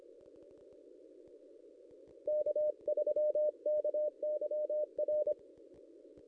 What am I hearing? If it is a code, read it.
Morse code, heard as a keyed tone, about 25 wpm, 600 Hz. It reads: K3KYR
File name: contests / ss-cw-k5tr-op-n5rz -1500